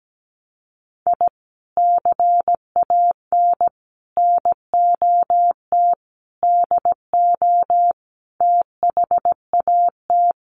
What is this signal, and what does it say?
Morse code, heard as a keyed tone, about 17 wpm, 700 Hz. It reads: I CAN NOT DO THAT
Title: Sets of 5 Words Encore2 17wpm